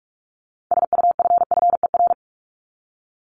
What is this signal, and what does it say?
Morse code, heard as a keyed tone, about 44 wpm, 700 Hz. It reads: SUFFER
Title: Top 1501-1750 Words 45wpm